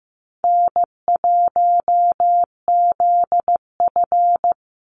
Morse code, heard as a keyed tone, about 15 wpm, 700 Hz. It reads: N1ZF
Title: NA SOTA Chaser Callsigns 15wpm